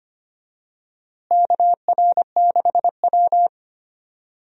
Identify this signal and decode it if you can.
Morse code, heard as a keyed tone, about 25 wpm, 700 Hz. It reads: KR6W
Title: USA Call Signs - 4 Characters - Repeated 2x - 25wpm